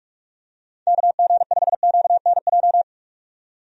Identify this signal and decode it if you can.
Morse code, heard as a keyed tone, about 46 wpm, 700 Hz. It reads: KG5QNJ
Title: USA Call Signs - 6 Characters - 45wpm